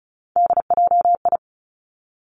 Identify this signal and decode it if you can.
Morse code, heard as a keyed tone, about 36 wpm, 700 Hz. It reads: DJI
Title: Three Challenging Letter-Number Combinations 35wpm